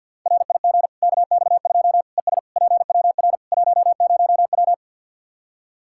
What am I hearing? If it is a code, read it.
Morse code, heard as a keyed tone, about 46 wpm, 700 Hz. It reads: RIG KX2 ES PWR 10W
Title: K1USN SST QSO Practice 50wpm